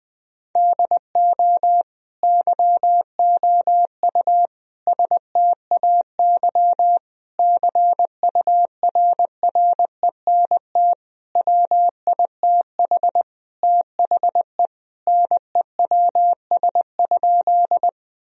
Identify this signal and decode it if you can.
Morse code, heard as a keyed tone, about 20 wpm, 700 Hz. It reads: DO YOU STAY CURRENT WITH THE NEWS?
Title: Sentences from Top 1000 Words - Review Part 4 - 20wpm